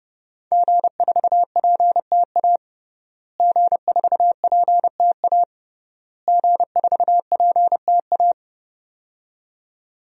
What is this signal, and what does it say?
Morse code, heard as a keyed tone, about 32 wpm, 700 Hz. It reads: G4PTA G4PTA G4PTA
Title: DXCC-100 Call Signs - 5 Characters - Repeated 3x - 30wpm